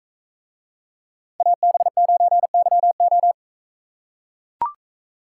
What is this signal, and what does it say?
Morse code, heard as a keyed tone, about 44 wpm, 700 Hz. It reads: AB9YO